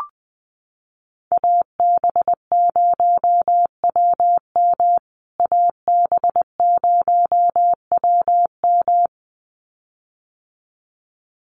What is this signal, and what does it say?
Morse code, heard as a keyed tone, about 20 wpm, 700 Hz. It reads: AB0WM AB0WM
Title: USA Call Signs Repeated 2x 20wpm